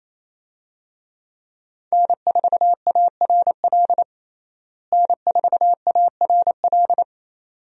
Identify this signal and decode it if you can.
Morse code, heard as a keyed tone, about 28 wpm, 700 Hz. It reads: N4ARL N4ARL
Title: USA Call Signs - 5 Characters - Repeated 2x - 28wpm